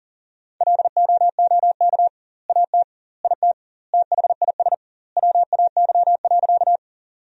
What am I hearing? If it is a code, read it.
Morse code, heard as a keyed tone, about 40 wpm, 700 Hz. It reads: LOOK AT IT THIS WAY.
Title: 2x Word-Spacing - Sentences from Top 100 Words - 40wpm